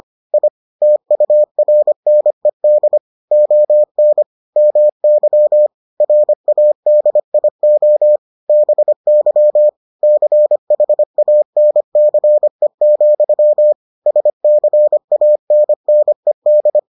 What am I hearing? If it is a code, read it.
Morse code, heard as a keyed tone, about 25 wpm, 600 Hz. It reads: I TURNED ON MY RADIO BY CHANCE, SCANNED